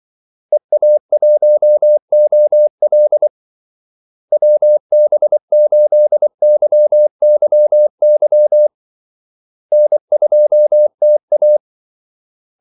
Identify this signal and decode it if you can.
Morse code, heard as a keyed tone, about 24 wpm, 600 Hz. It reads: EA1OL WB8YYY N2TA
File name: Callsigns-24wpm